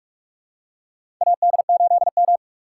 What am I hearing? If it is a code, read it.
Morse code, heard as a keyed tone, about 46 wpm, 700 Hz. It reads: AD8M
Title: USA Call Signs - 4 Characters - 4x Character Spacing - 45wpm